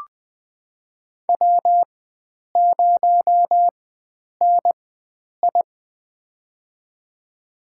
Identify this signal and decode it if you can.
Morse code, heard as a keyed tone, about 20 wpm, spaced as for 9 wpm, 700 Hz. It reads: W0NI
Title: USA Call Signs - 4 Characters - 4x Character Spacing - 20wpm